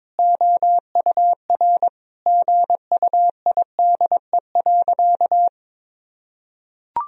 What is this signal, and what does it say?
Morse code, heard as a keyed tone, about 22 wpm, 700 Hz. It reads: OUR GUIDE.
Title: Sentences from Top 1000 Words - Review Part 3 - 22wpm